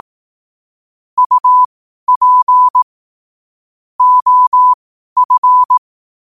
Morse code, heard as a keyed tone, about 18 wpm, spaced as for 12 wpm, 1000 Hz. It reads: UP OF
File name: qotdfiles / 12wpm